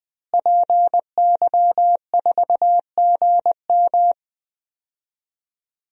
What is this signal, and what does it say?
Morse code, heard as a keyed tone, about 20 wpm, 700 Hz. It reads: PY4GM
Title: DXCC-100 Call Signs - 5 Characters - Repeated 3x - 20wpm